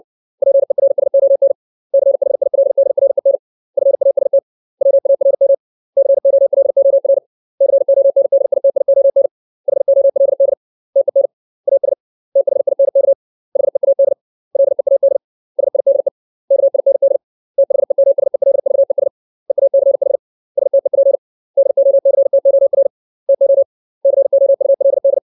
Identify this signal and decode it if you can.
Morse code, heard as a keyed tone, about 60 wpm, 550 Hz. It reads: PERSON CHILDREN FAST WARM COLOR CONDITION HOLD TEN AS THINK HEAD LEAD SIDE KIND THEMSELVES EACH STEP DOCTOR TO COULD
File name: 632 words / 632 Words 60WPM